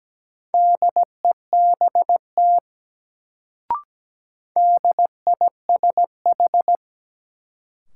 Morse code, heard as a keyed tone, about 17 wpm, 700 Hz. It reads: DEBT DISH
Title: Sentences from Top 1300 Words - New Word Drill - 17wpm